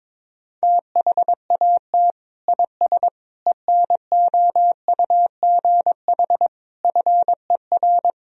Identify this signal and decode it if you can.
Morse code, heard as a keyed tone, about 22 wpm, 700 Hz. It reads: THAT IS ENOUGH FER
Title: Sentences from Top 400 Words and Abbreviations 22wpm